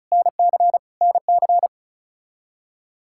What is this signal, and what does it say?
Morse code, heard as a keyed tone, about 36 wpm, 700 Hz. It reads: NC NC
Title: K1USN SST QSO Practice 35wpm